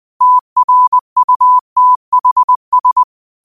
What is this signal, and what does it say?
Morse code, heard as a keyed tone, about 20 wpm, 1000 Hz. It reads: TRUTHS
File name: qotdfiles / 20wpm